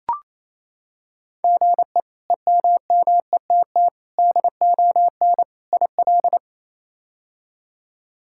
Morse code, heard as a keyed tone, about 28 wpm, 700 Hz. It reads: GE EMMETT DON IL
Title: K1USN SST QSO Practice - 28wpm